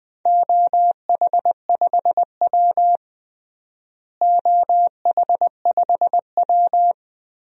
Morse code, heard as a keyed tone, about 20 wpm, 700 Hz. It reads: OH5W OH5W
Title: DXCC-100 Call Signs - 4 Characters - Repeated 3x - 20wpm